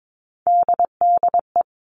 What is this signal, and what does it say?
Morse code, heard as a keyed tone, about 22 wpm, 700 Hz. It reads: DDE
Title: All Cut Numbers - 3-Digit 22wpm